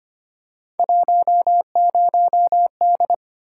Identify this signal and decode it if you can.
Morse code, heard as a keyed tone, about 25 wpm, 700 Hz. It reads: 10D
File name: Three Letter-Number Combinations - 25wpm